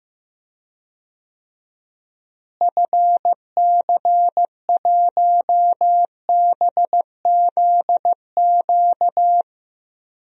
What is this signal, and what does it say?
Morse code, heard as a keyed tone, about 15 wpm, 700 Hz. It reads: FC1BZQ
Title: DXCC-100 Call Signs - 6 Characters - Repeated 2x - 15wpm